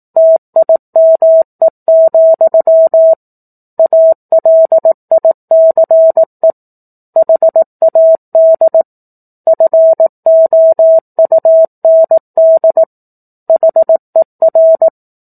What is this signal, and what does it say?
Morse code, heard as a keyed tone, about 18 wpm, 650 Hz. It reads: TIME, ALICE HAD FOUND HER
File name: 18 WPM